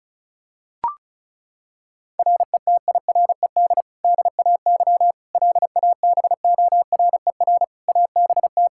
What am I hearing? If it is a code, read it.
Morse code, heard as a keyed tone, about 36 wpm, 700 Hz. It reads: RETIRED DAY LABORER ABT
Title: Practice QSO - Exchange 3 - More Info - 35wpm